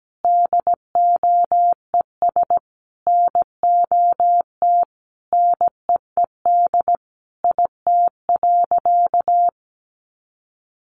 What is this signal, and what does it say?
Morse code, heard as a keyed tone, about 17 wpm, 700 Hz. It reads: DOES NOT NEED IT.
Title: Sentences from Top 1400 Words 17wpm